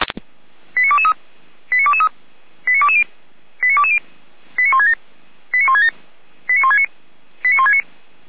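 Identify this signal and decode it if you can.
Five-tone selective call (ZVEI), heard as quick runs of five tones, tones of about 70 ms. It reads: 89223 89223 89220 89220 89167 89167 89169 89169
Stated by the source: ZVEI five-tone call 89123 received by SDR, from audio.f85360.feuerwehr-orlamuende.org fire brigade server.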